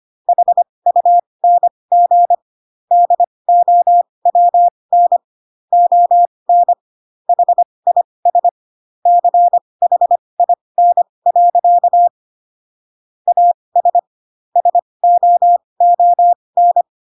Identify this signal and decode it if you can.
Morse code, heard as a keyed tone, about 25 wpm, spaced as for 20 wpm, 700 Hz. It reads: HUNG DOWN ON HIS CHIN. AS SOON
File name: Pinocchio / Pinocchio ch1 15 25 ik1hge